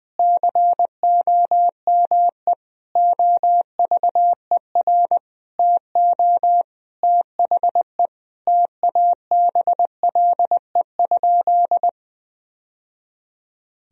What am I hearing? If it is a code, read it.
Morse code, heard as a keyed tone, about 20 wpm, 700 Hz. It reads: COME OVER TO THE TABLE?
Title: Sentences from Top 1000 Words - Review Part 2 - 20wpm